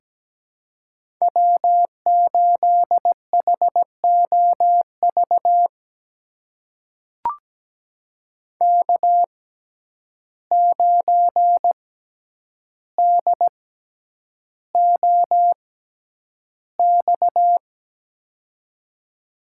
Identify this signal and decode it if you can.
Morse code, heard as a keyed tone, about 17 wpm, spaced as for 9 wpm, 700 Hz. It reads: W8HOV K9DOX
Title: USA Call Signs - 5 Characters - 6x Character Spacing - 17wpm